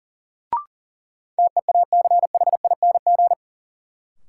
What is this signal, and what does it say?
Morse code, heard as a keyed tone, about 40 wpm, 700 Hz. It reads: TEACHING
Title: Sentences from Top 1100 Words - New Word Drill - 40wpm